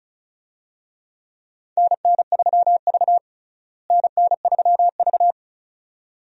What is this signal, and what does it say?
Morse code, heard as a keyed tone, about 34 wpm, 700 Hz. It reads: NN3V NN3V
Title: USA Call Signs - 4 Characters - Repeated 2x - 35wpm